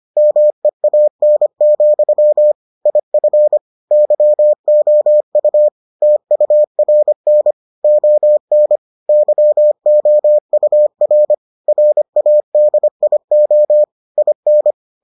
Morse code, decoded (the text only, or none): MEAN, IF YOU TURN ON YOUR RADIO IN